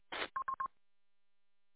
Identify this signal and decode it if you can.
There is no code.